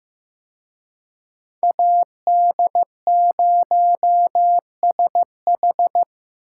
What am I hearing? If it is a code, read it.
Morse code, heard as a keyed tone, about 18 wpm, 700 Hz. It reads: AD0SH
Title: Speed Racing - USA Call Signs 15wpm